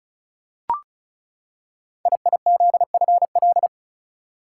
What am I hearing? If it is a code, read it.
Morse code, heard as a keyed tone, about 36 wpm, 700 Hz. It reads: IIZFL
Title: Code Groups - 5 Letters - 35wpm